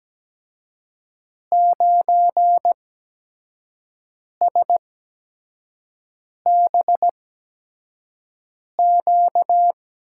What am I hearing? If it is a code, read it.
Morse code, heard as a keyed tone, about 17 wpm, spaced as for 5 wpm, 700 Hz. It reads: 9SBQ
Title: USA Call Signs - 5 Characters - 8x Character Spacing - 17wpm